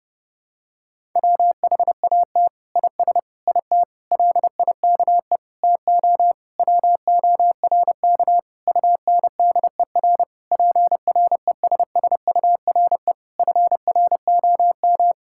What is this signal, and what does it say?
Morse code, heard as a keyed tone, about 32 wpm, 700 Hz. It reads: WHAT IS IT LIKE TO WORK UNDER PRESSURE FROM
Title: Sentences from Top 1000 Words - Review Part 6 - 30wpm